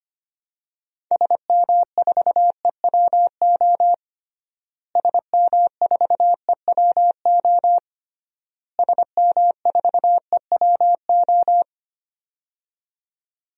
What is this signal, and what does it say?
Morse code, heard as a keyed tone, about 25 wpm, 700 Hz. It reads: SM4EWO SM4EWO SM4EWO
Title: DXCC-100 Call Signs - 6 Characters - Repeated 3x - 25wpm